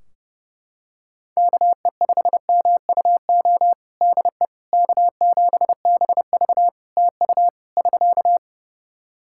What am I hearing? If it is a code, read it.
Morse code, heard as a keyed tone, about 32 wpm, 700 Hz. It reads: KE5MUO DE K7BV TU <SK>
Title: K1USN SST QSO Practice 30wpm